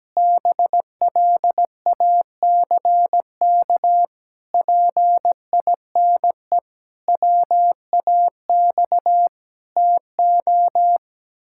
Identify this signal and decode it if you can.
Morse code, heard as a keyed tone, about 17 wpm, 700 Hz. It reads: BLACK PINE WAX TO
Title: Pangram Challenge v2 - 17wpm